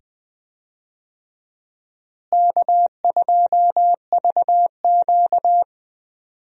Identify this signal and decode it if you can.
Morse code, heard as a keyed tone, about 20 wpm, 700 Hz. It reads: K2VQ
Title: USA Call Signs - 4 Characters - Repeated 2x - 20wpm